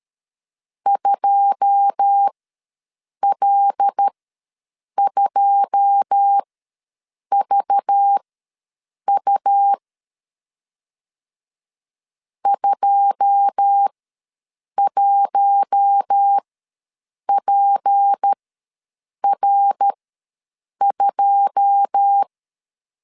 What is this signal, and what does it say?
Morse code, heard as a keyed tone, about 13 wpm, spaced as for 7 wpm, 800 Hz. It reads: 2L2VU 21PR2
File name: LEZIONI / 9 2 ws 12wpm 3406